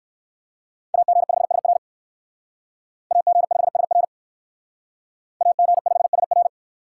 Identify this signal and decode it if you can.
Morse code, heard as a keyed tone, about 59 wpm, 700 Hz. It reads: AG5SR AG5SR AG5SR